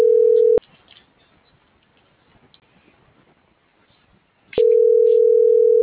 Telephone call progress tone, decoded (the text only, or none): ringback tone